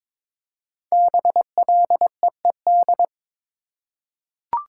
Morse code, heard as a keyed tone, about 22 wpm, 700 Hz. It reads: BLEED
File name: 5 Letter Words - First 3 Letters Given - 22wpm